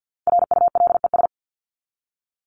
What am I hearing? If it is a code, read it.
Morse code, heard as a keyed tone, about 50 wpm, 700 Hz. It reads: RULES
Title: ES:S Suffix - Top 100 Words with Base Words 50wpm